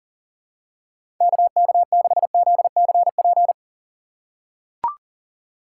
Morse code, heard as a keyed tone, about 40 wpm, 700 Hz. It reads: KK6ZCP
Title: USA Call Signs - 6 Characters - 2x Character Spacing - 40wpm